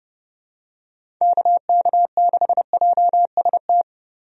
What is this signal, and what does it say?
Morse code, heard as a keyed tone, about 32 wpm, 700 Hz. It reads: KK6JST
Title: USA Call Signs - 6 Characters - Repeated 2x - 30wpm